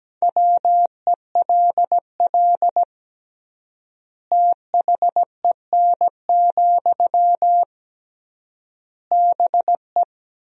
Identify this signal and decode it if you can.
Morse code, heard as a keyed tone, about 17 wpm, 700 Hz. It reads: WELL THEN, BE
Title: 3x Word-Spacing - Sentences from Top 500 Words - 17wpm